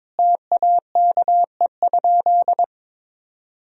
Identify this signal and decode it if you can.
Morse code, heard as a keyed tone, about 22 wpm, 700 Hz. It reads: TAKE?
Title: Sentences from Top 1000 Words - Review Part 4 - 22wpm